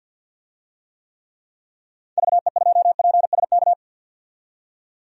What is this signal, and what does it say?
Morse code, heard as a keyed tone, about 52 wpm, 700 Hz. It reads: VE2PSK
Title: DXCC-100 Call Signs - 6 Characters - 50wpm